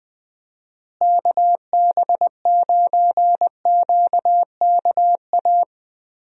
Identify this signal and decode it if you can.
Morse code, heard as a keyed tone, about 20 wpm, 700 Hz. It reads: KB9QKA
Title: USA Call Signs - 6 Characters - Repeated 2x - 20wpm